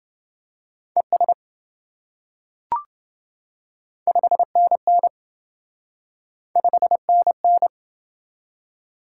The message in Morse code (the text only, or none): ES 5NN 5NN